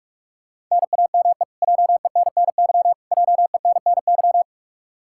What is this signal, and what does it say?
Morse code, heard as a keyed tone, about 46 wpm, 700 Hz. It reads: NAME JENNY JENNY
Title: Practice QSO - Exchange 1 - Beginning - 45wpm